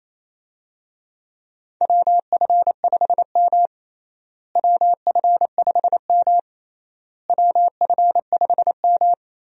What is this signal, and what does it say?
Morse code, heard as a keyed tone, about 28 wpm, 700 Hz. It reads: WF5M WF5M WF5M
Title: USA Call Signs - 4 Characters - Repeated 3x - 28wpm